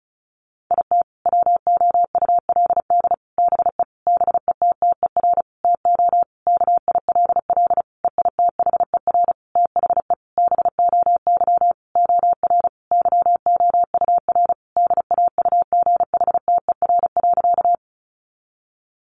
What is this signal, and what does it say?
Morse code, heard as a keyed tone, about 34 wpm, 700 Hz. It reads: IT WOULD BE BETTER TO KILL EITHER THE BOY OR YOUR DAUGHTER.
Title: Sentences from Top 1500 Words 35wpm